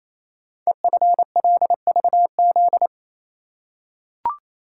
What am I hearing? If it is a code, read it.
Morse code, heard as a keyed tone, about 28 wpm, 700 Hz. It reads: EFLVZ
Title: Code Groups - 5 Letters - 28wpm